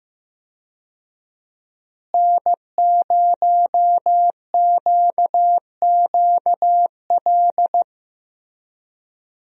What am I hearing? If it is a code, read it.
Morse code, heard as a keyed tone, about 15 wpm, 700 Hz. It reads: N0QQL
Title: USA Call Signs - 5 Characters - Repeated 3x - 15wpm